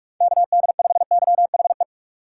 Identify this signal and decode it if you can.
Morse code, heard as a keyed tone, about 45 wpm, 700 Hz. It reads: KD5YHE